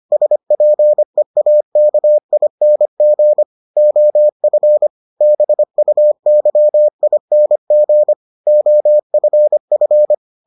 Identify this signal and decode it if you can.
Morse code, heard as a keyed tone, about 25 wpm, 600 Hz. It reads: SPEAKING OF BUYING OFF